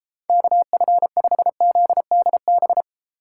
Morse code, heard as a keyed tone, about 34 wpm, 700 Hz. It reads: KF5ZDB